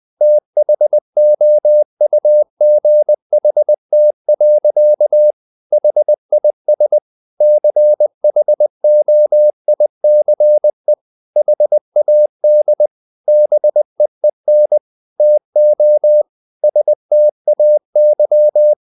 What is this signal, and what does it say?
Morse code, heard as a keyed tone, about 20 wpm, 600 Hz. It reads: THOUGHT. HIS CHOICE HAD BEEN TO STAY
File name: the old man and the sea / the old man and the sea part 1 20wpm